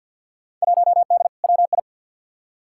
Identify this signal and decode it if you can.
Morse code, heard as a keyed tone, about 50 wpm, 700 Hz. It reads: 1D WI
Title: Field Day Exchanges - 50wpm